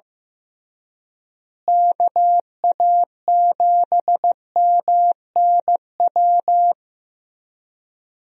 Morse code, heard as a keyed tone, about 15 wpm, 700 Hz. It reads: KA7MNW
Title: USA Call Signs - 6 Characters - Repeated 3x - 15wpm